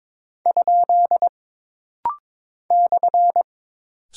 Morse code, heard as a keyed tone, about 22 wpm, 700 Hz. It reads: ? /
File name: practice / 22wpm / Single Punctuation-Prosign - Rapid-Fire - 22wpm